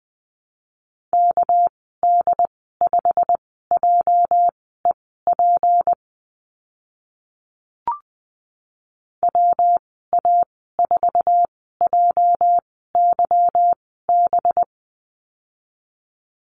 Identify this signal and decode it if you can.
Morse code, heard as a keyed tone, about 20 wpm, spaced as for 14 wpm, 700 Hz. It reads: KD5JEP WA4JYB
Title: 213 Lesson 34g - Number 8 - Call Signs 20wpm